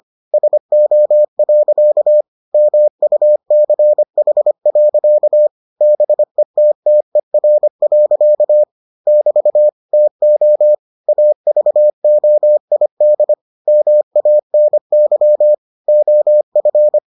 Morse code, heard as a keyed tone, about 25 wpm, 600 Hz. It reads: SO. MUCH. BETTER. <BT> TO AVOID MANY OF